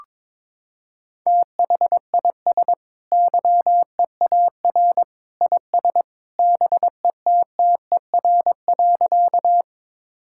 Morse code, heard as a keyed tone, about 22 wpm, 700 Hz. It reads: THIS YEAR IS BETTER.
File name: practice / 22wpm / Sentences with 2-Word Phrases - 22wpm